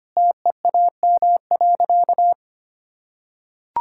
Morse code, heard as a keyed tone, about 25 wpm, 700 Hz. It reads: TEAM.
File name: practice / 25wpm / Sentences with 3-Word Phrases - 25wpm